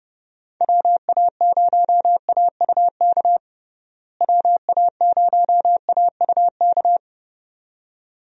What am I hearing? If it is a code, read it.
Morse code, heard as a keyed tone, about 30 wpm, 700 Hz. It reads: WA0AUK WA0AUK
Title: USA Call Signs - 6 Characters - Repeated 2x - 30wpm